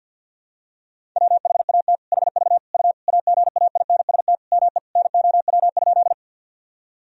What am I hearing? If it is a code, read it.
Morse code, heard as a keyed tone, about 50 wpm, 700 Hz. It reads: WHAT HV U AGAINST ME NOW?
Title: Sentences from Top 400 Words and Abbreviations 50wpm